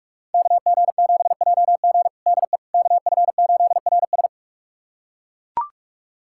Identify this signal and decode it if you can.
Morse code, heard as a keyed tone, about 41 wpm, 700 Hz. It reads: KG7JG DE KF8RS
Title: USA Callsign DE Callsign - 5 Characters - 45wpm